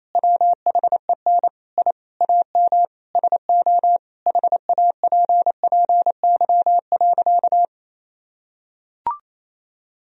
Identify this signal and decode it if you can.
Morse code, heard as a keyed tone, about 28 wpm, 700 Hz. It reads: WHEN I AM SO HAPPY.
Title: Sentences from Top 1000 Words - Review Part 2 - 28wpm